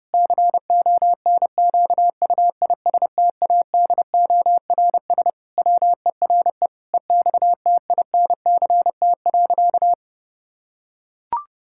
Morse code, heard as a keyed tone, about 30 wpm, 700 Hz. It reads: CONQUISTADORS WERE EXTINCT.
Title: Pangram Challenge v2 - 30wpm